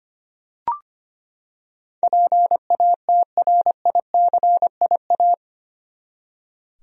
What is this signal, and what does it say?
Morse code, heard as a keyed tone, about 25 wpm, 700 Hz. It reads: PATRICIA
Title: USA Common Names - 25wpm